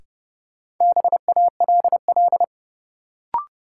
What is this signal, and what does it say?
Morse code, heard as a keyed tone, about 30 wpm, 700 Hz. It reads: BALL
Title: Sentences from Top 900 Words - New Word Drill - 30wpm